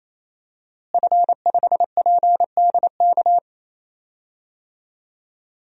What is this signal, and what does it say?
Morse code, heard as a keyed tone, about 28 wpm, 700 Hz. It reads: F5PDK
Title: DXCC-100 Call Signs - 5 Characters - Repeated 3x - 28wpm